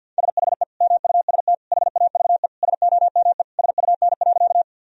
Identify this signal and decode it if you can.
Morse code, heard as a keyed tone, about 49 wpm, 700 Hz. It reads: SHE MUST HAVE SOME SUN.
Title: Sentences from Top 1000 Words - Review Part 1 - 50wpm